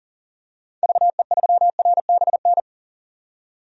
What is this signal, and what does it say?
Morse code, heard as a keyed tone, about 37 wpm, 700 Hz. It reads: VE3RBN
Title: DXCC-100 Call Signs - 6 Characters - 40wpm